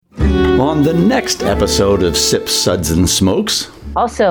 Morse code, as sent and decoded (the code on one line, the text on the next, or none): none